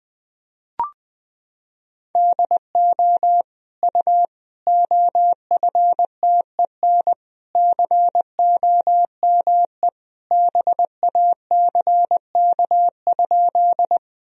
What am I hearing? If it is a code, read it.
Morse code, heard as a keyed tone, about 21 wpm, 700 Hz. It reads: DO U OFTEN COME BACK?
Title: Sentences from Top 300 Words and Abbreviations 20wpm